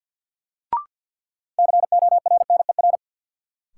Morse code, heard as a keyed tone, about 52 wpm, 700 Hz. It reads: CORNER
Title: Sets of 5 Words - New Word Drill - 50wpm